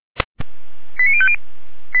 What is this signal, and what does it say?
Five-tone selective call (ZVEI), heard as quick runs of five tones, tones of about 70 ms. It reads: 89050
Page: feuerwehr-orlamuende